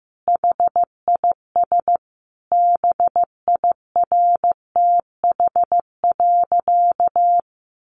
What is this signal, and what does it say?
Morse code, heard as a keyed tone, about 15 wpm, 700 Hz. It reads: HIS BIRTH.
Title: Sentences from Top 900 Words 15wpm